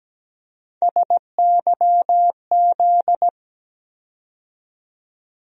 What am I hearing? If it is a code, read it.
Morse code, heard as a keyed tone, about 17 wpm, 700 Hz. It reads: SYZ
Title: USA Call Sign Postfixes - 3 Letters 17wpm